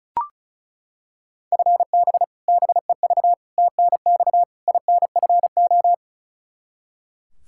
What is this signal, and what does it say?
Morse code, heard as a keyed tone, about 36 wpm, 700 Hz. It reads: FB BEV TNX INFO
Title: Practice QSO - Exchange 3 - More Info - 35wpm